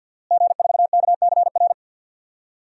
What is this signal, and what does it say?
Morse code, heard as a keyed tone, about 51 wpm, 700 Hz. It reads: G4KCR
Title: DXCC-100 Call Signs - 5 Characters - Repeated 2x - 50wpm